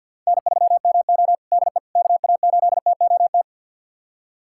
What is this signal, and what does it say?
Morse code, heard as a keyed tone, about 47 wpm, 700 Hz. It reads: N3MO DE KA8TOT
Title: USA Callsign DE Callsign 50wpm